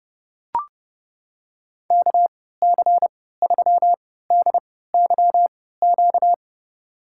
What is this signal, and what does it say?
Morse code, heard as a keyed tone, about 30 wpm, spaced as for 17 wpm, 700 Hz. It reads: KC3DYQ
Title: USA Call Signs - 3x Character Spacing - 30wpm